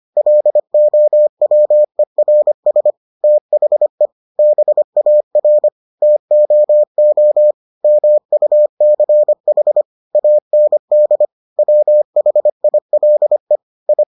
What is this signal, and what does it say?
Morse code, heard as a keyed tone, about 25 wpm, 600 Hz. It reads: LOWERS THE BAR TOO MUCH AND WHILE I